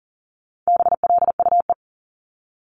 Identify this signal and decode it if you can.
Morse code, heard as a keyed tone, about 40 wpm, 700 Hz. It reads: BLUE
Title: Multi-letter Phonemes 40wpm